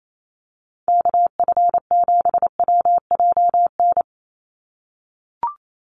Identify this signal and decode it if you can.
Morse code, heard as a keyed tone, about 28 wpm, 700 Hz. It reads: KF7WJN
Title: USA Call Signs 28wpm